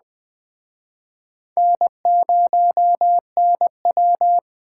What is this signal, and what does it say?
Morse code, heard as a keyed tone, about 21 wpm, 700 Hz. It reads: N0NW